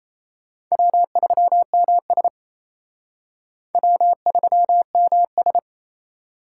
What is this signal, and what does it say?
Morse code, heard as a keyed tone, about 36 wpm, 700 Hz. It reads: W3MS W3MS